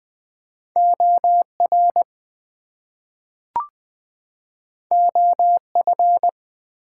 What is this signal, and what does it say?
Morse code, heard as a keyed tone, about 20 wpm, 700 Hz. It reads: OR OF
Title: Two Letter Words - 20wpm